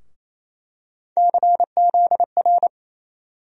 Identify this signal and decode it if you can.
Morse code, heard as a keyed tone, about 28 wpm, 700 Hz. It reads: CZR
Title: USA Call Sign Postfixes - 3 Letters 28wpm